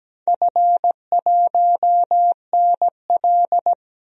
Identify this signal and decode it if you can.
Morse code, heard as a keyed tone, about 17 wpm, 700 Hz. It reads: F1NL
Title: DXCC-100 Call Signs - 4 Characters - 17wpm